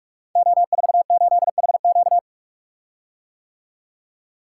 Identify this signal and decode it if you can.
Morse code, heard as a keyed tone, about 44 wpm, 700 Hz. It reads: O48HQ
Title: Code Groups - 5 Letter:Numbers - 45wpm